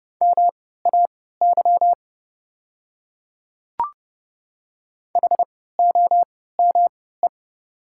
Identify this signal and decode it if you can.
Morse code, heard as a keyed tone, about 30 wpm, spaced as for 17 wpm, 690 Hz. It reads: MAY HOME
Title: 3x Character Spacing - Top 200 Words - 30wpm